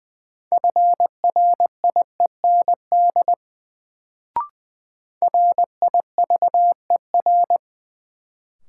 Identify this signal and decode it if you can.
Morse code, heard as a keyed tone, about 20 wpm, 700 Hz. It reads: FRIEND RIVER
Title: Sentences from Top 300 Words - New Word Drill - 20wpm